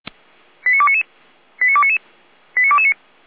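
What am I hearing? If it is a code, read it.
Five-tone selective call (ZVEI), heard as quick runs of five tones, tones of about 70 ms. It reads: 89299 89299 89229
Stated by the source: ZVEI five-tone selective call received on SDR, fire brigade Feuerwehr Orlamünde.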